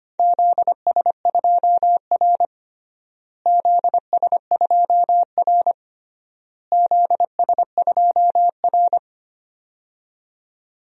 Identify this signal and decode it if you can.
Morse code, heard as a keyed tone, about 25 wpm, 700 Hz. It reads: ZS2R ZS2R ZS2R
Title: DXCC-100 Call Signs - 4 Characters - Repeated 3x - 25wpm